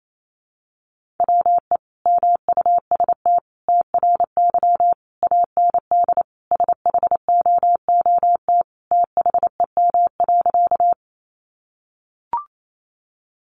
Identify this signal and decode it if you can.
Morse code, heard as a keyed tone, about 28 wpm, 700 Hz. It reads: WE MUST TRY AND SHOOT THEM.
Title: Sentences from Top 1200 Words 28wpm